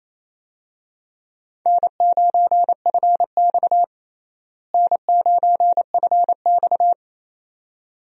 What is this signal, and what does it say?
Morse code, heard as a keyed tone, about 27 wpm, 700 Hz. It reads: N9FX N9FX
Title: USA Call Signs - 4 Characters - Repeated 2x - 28wpm